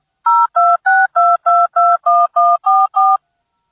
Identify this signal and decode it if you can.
DTMF, heard as telephone touch tones, tones of about 220 ms. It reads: 0262221144